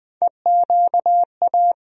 Morse code, heard as a keyed tone, about 20 wpm, 700 Hz. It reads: EQA
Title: USA Call Sign Postfixes - 3 Letters 20wpm